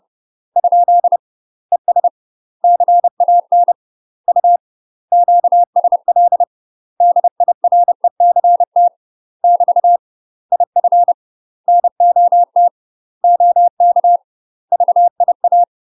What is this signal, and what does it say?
Morse code, heard as a keyed tone, about 30 wpm, 700 Hz. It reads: ? ES CAN U QSL DIRECT <BT> IF NOT OK VIA
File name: UK MASTER QSOs TRAINING / 30WPM / UK MASTER QSOs 0034